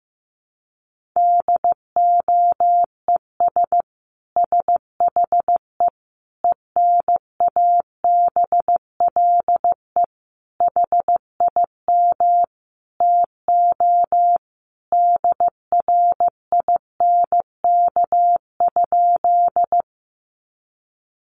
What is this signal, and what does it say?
Morse code, heard as a keyed tone, about 15 wpm, 700 Hz. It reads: DOES SHE ENABLE HIM TO DRINK?
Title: Sentences from Top 1500 Words 15wpm